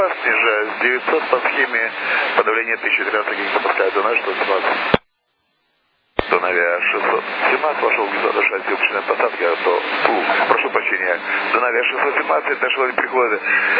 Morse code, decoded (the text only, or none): none